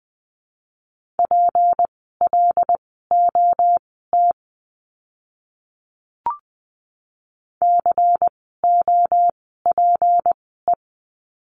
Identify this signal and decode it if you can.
Morse code, heard as a keyed tone, about 20 wpm, spaced as for 14 wpm, 700 Hz. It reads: PLOT COPE
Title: 153 Lesson 25f - Letter p - Words 20wpm